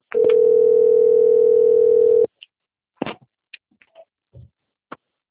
Ringback tone (North America), 440 + 480 Hz.